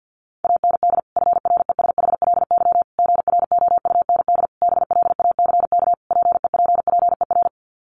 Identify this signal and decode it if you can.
Morse code, heard as a keyed tone, about 50 wpm, 700 Hz. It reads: AND FRESHLY GROUND BLACK PEPPER
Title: Sets of 5 Words Encore 50wpm